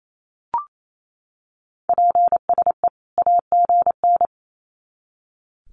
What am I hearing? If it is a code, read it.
Morse code, heard as a keyed tone, about 28 wpm, 700 Hz. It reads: PSE AGN
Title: Common QSO Phrases 28wpm